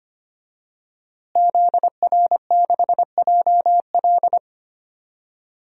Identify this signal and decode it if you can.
Morse code, heard as a keyed tone, about 25 wpm, 700 Hz. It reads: ZR6JL
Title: DXCC-100 Call Signs - 5 Characters - Repeated 2x - 25wpm